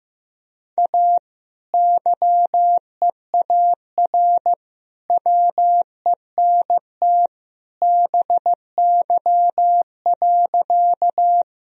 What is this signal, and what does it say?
Morse code, heard as a keyed tone, about 15 wpm, 700 Hz. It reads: A YEAR WENT BY.
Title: Sentences from Top 1000 Words - Review Part 1 - 15wpm